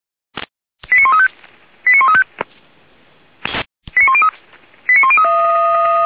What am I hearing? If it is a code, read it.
Five-tone selective call (ZVEI), heard as quick runs of five tones, tones of about 80 ms. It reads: 89135 89135 89103 89103